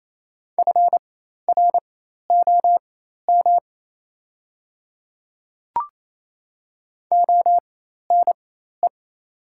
Morse code, heard as a keyed tone, about 27 wpm, spaced as for 13 wpm, 700 Hz. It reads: FROM ONE
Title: 4x Character Spacing - Top 300 Words - 28wpm